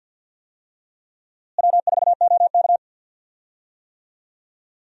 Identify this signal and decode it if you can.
Morse code, heard as a keyed tone, about 48 wpm, 700 Hz. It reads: W4OK
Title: USA Call Signs - 4 Characters - Repeated 3x - 50wpm